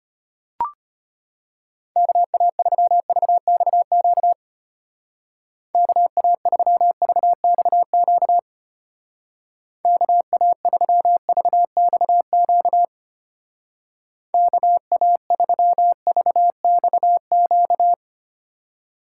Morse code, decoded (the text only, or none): KA3VXQ KA3VXQ KA3VXQ KA3VXQ